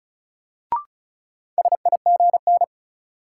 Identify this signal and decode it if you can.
Morse code, heard as a keyed tone, about 36 wpm, 700 Hz. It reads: SIGN
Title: Sentences from Top 1000 Words - New Word Drill - 35wpm